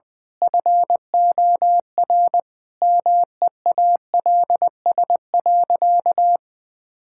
Morse code, heard as a keyed tone, about 20 wpm, 700 Hz. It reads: FOR MEALS.
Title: Pangram Challenge v2 - 20wpm